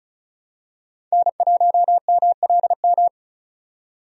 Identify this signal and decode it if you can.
Morse code, heard as a keyed tone, about 36 wpm, 700 Hz. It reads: N1MLM
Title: USA Call Signs - 5 Characters - 35wpm